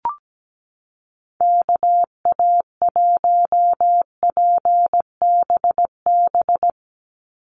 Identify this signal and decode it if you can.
Morse code, heard as a keyed tone, about 17 wpm, 700 Hz. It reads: KA1PBB
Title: USA Call Signs 17wpm